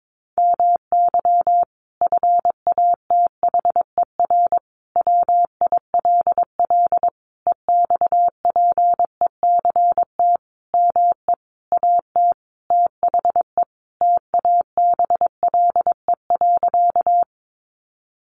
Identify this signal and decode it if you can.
Morse code, heard as a keyed tone, about 22 wpm, 700 Hz. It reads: MY FATHER WILL EXPECT ME AT THE TABLE.
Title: Sentences from Top 1100 Words 22wpm